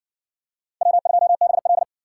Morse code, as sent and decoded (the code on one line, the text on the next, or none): .-- ..--- -... .-..
W2BL